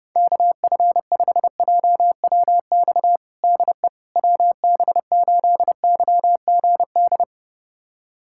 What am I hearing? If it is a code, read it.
Morse code, heard as a keyed tone, about 32 wpm, 700 Hz. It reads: KF5JWX DE WB8YGD
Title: USA Callsign DE Callsign 30wpm